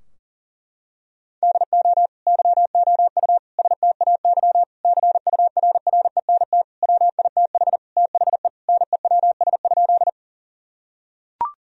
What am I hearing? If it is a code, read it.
Morse code, heard as a keyed tone, about 37 wpm, 700 Hz. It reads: DO YOU STAY CURRENT WITH THE NEWS?
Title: Sentences from Top 1000 Words - Review Part 4 - 40wpm